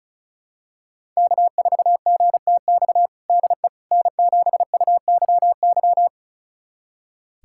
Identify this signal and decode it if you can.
Morse code, heard as a keyed tone, about 34 wpm, 700 Hz. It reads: K4GTX DE N7UYY
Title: USA Callsign DE Callsign 35wpm